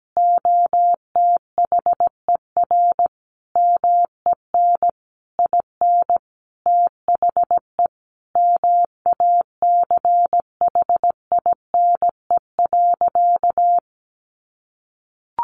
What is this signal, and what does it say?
Morse code, heard as a keyed tone, about 17 wpm, 700 Hz. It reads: OTHER MEN IN THE MACHINE.
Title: A Sound Of Thunder 17wpm